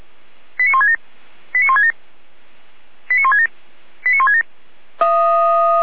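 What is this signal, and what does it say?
Five-tone selective call (ZVEI), heard as quick runs of five tones, tones of about 70 ms. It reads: 89167 89167 89167 89167
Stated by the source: ZVEI five-tone selective call received with an SDR, from the Feuerwehr Orlamünde fire brigade audio server.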